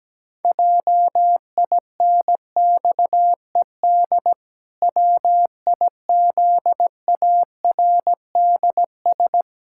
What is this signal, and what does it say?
Morse code, heard as a keyed tone, about 17 wpm, 700 Hz. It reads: JINXED WIZARDS
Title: Pangram Challenge v2 - 17wpm